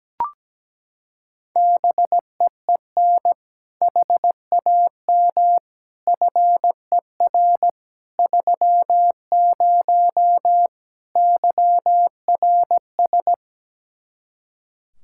Morse code, heard as a keyed tone, about 17 wpm, 700 Hz. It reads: BEEN HAM FER 30 YRS
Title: Practice QSO - Exchange 3 - More Info - 17wpm